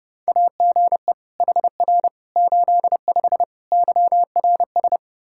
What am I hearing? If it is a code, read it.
Morse code, heard as a keyed tone, about 32 wpm, 700 Hz. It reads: AGE HR 85 YRS
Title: Practice QSO - Exchange 3 - More Info - 30wpm